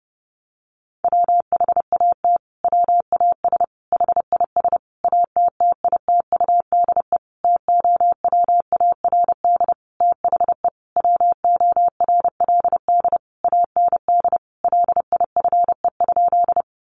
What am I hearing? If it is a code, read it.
Morse code, heard as a keyed tone, about 30 wpm, 700 Hz. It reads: WHAT WAS HIS ATTITUDE TOWARD THE WORLD AND LIFE?
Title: Sentences from Top 1300 Words 30wpm